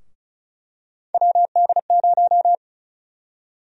Morse code, heard as a keyed tone, about 34 wpm, 700 Hz. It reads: WD0
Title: USA Call Sign Prefixes - 35wpm